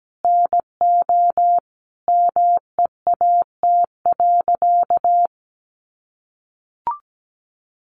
Morse code, heard as a keyed tone, about 17 wpm, 700 Hz. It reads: NO MEAT.
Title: Sentences from Top 1300 Words 17wpm